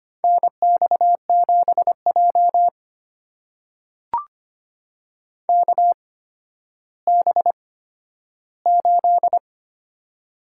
Morse code, heard as a keyed tone, about 25 wpm, spaced as for 12 wpm, 700 Hz. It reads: NX7J KB8
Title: USA Call Signs - 8x Character Spacing - 25wpm